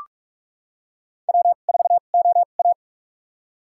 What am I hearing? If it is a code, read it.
Morse code, heard as a keyed tone, about 46 wpm, spaced as for 33 wpm, 700 Hz. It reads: W4OA
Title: USA Call Signs - 2x Character Spacing - 45wpm